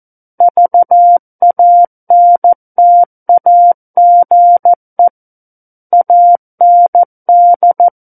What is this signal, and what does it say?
Morse code, heard as a keyed tone, about 14 wpm, 700 Hz. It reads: VANTAGE AND
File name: Book One / 05-Chapter 5 - 14 WPM